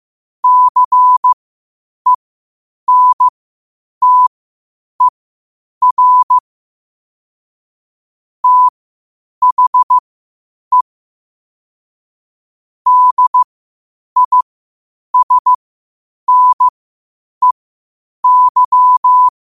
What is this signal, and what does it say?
Morse code, heard as a keyed tone, about 15 wpm, spaced as for 8 wpm, 1000 Hz. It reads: CENTER THE DISNEY
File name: qotdfiles / 8wpm